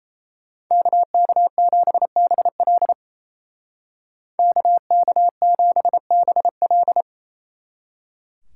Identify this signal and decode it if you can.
Morse code, heard as a keyed tone, about 33 wpm, 700 Hz. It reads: KK7BL KK7BL
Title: Speed Racing - USA 2x2 Call Signs 28wpm